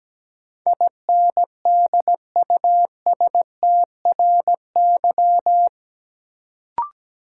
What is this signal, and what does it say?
Morse code, heard as a keyed tone, about 17 wpm, 700 Hz. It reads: INDUSTRY
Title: Sentences from Top 1000 Words - Review Part 4 - New Word Drill - 17wpm